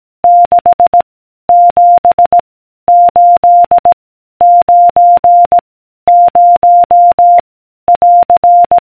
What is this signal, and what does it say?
Morse code, heard as a keyed tone, about 18 wpm, 700 Hz. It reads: 6 7 8 9 0 <AR>